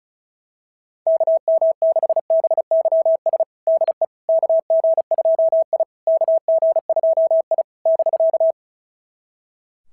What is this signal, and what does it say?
Morse code, heard as a keyed tone, about 34 wpm, 660 Hz. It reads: KM6BYS DE KG2I KG2I <BK>
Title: SOTA QSO Practice 35wpm